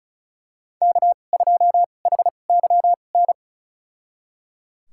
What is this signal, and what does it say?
Morse code, heard as a keyed tone, about 36 wpm, spaced as for 25 wpm, 700 Hz. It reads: K2HYN
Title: USA Call Signs - 5 Characters - 2x Character Spacing - 35wpm